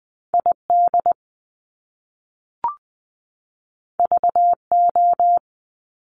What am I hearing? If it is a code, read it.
Morse code, heard as a keyed tone, about 20 wpm, 700 Hz. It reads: ID VO
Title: Top Two Letter Combinations in Words 20wpm